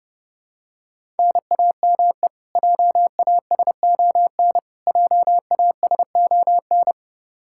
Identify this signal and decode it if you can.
Morse code, heard as a keyed tone, about 30 wpm, 700 Hz. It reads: NAME JASON JASON
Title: K1USN SST QSO Practice 30wpm